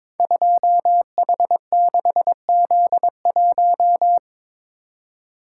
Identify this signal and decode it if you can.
Morse code, heard as a keyed tone, about 23 wpm, 700 Hz. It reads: 2H6Z1
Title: Code Groups - 5 Letter:Numbers - 22wpm